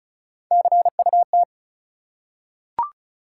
Morse code, heard as a keyed tone, about 36 wpm, 700 Hz. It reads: CUT
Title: Three Letter Words - 35wpm